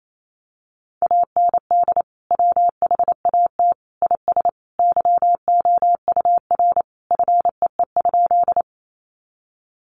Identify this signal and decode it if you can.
Morse code, heard as a keyed tone, about 28 wpm, 700 Hz. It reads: AND WHAT IS YOUR FEE?
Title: Sentences from Top 1500 Words 28wpm